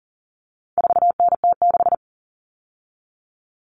Morse code, heard as a keyed tone, about 40 wpm, 700 Hz. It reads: 4NT6
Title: Common Cut Numbers - 4-Digit 40wpm